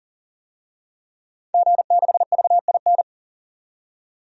Morse code, heard as a keyed tone, about 37 wpm, 700 Hz. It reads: G6VIN
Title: DXCC-100 Call Signs - 5 Characters - 40wpm